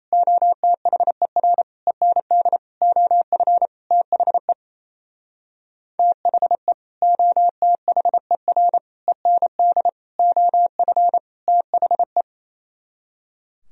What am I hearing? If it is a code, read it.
Morse code, heard as a keyed tone, about 34 wpm, 700 Hz. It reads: OTHER END OF THE THE OTHER END OF THE
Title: Speed-Racing - Sets of 5 Words 28wpm